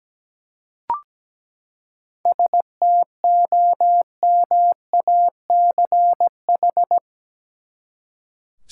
Morse code, heard as a keyed tone, about 17 wpm, 700 Hz. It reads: STOMACH
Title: Seven Letter Words - 17wpm